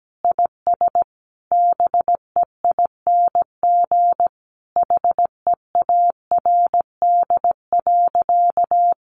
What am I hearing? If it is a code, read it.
Morse code, heard as a keyed tone, about 17 wpm, 700 Hz. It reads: IS BEING HEARD.
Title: Sentences from Top 1500 Words 17wpm